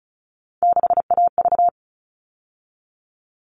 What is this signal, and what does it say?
Morse code, heard as a keyed tone, about 34 wpm, 700 Hz. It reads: 6AV
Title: All Cut Numbers - 3-Digit 35wpm